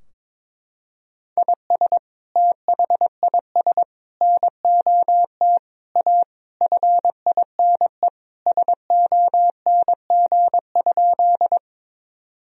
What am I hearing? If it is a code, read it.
Morse code, heard as a keyed tone, about 22 wpm, 700 Hz. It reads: IS THIS NOT A FINE SONG?
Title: Sentences from Top 1000 Words - Review Part 2 - 22wpm